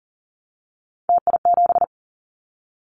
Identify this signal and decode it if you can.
Morse code, heard as a keyed tone, about 40 wpm, 700 Hz. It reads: TI7
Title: Callsign Prefixes for Top 100 DXCC - 40wpm